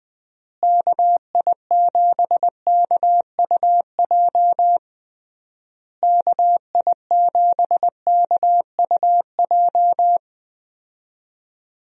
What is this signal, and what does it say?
Morse code, heard as a keyed tone, about 21 wpm, 700 Hz. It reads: KI7KUJ KI7KUJ
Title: USA Call Signs - 6 Characters - Repeated 3x - 20wpm